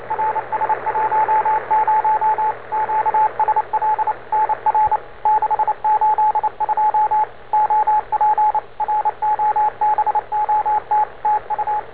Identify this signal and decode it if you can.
Morse code, heard as a keyed tone, about 29 wpm, 870 Hz, with radio noise. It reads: RS10 QSL NR 682 OP ROBOT TU
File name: n4st / rs10 robot aa3o